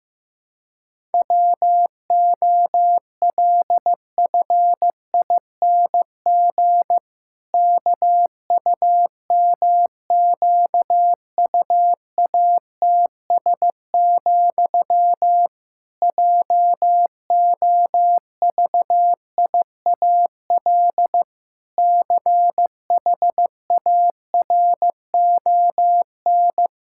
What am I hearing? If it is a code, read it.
Morse code, heard as a keyed tone, about 16 wpm, 700 Hz. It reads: WOLFING KUMQUATS, JOVIAL CHARON
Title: Pangram Challenge v2 - 15wpm